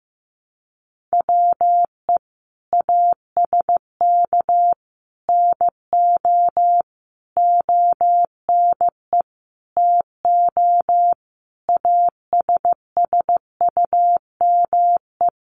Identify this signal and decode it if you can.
Morse code, heard as a keyed tone, about 15 wpm, 700 Hz. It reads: WE ASK NO ONE TO ASSUME
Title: Sentences from Top 1500 Words 15wpm